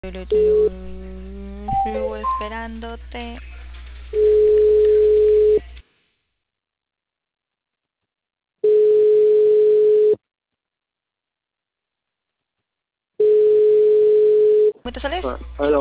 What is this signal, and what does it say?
Ringback tone (425 Hz, 1.5 s on, 3 s off), with speech.